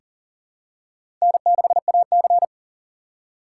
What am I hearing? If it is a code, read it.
Morse code, heard as a keyed tone, about 40 wpm, 700 Hz. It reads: N6AC
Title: USA Call Signs - 4 Characters - 4x Character Spacing - 40wpm